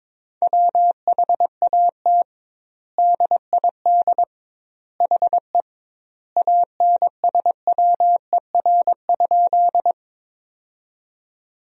Morse code, heard as a keyed tone, about 22 wpm, 700 Hz. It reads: WHAT DID HE ANSWER?
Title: 2x Word-Spacing - Sentences from Top 300 Words - 22wpm